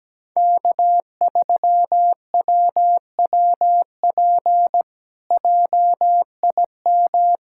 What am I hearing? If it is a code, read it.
Morse code, heard as a keyed tone, about 17 wpm, 700 Hz. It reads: K3WWP JIM
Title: K1USN SST QSO Practice - 17wpm